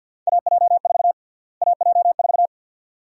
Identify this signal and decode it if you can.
Morse code, heard as a keyed tone, about 52 wpm, 700 Hz. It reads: AJ4 AJ4
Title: USA Call Sign Prefixes - Repeated 2x - 50wpm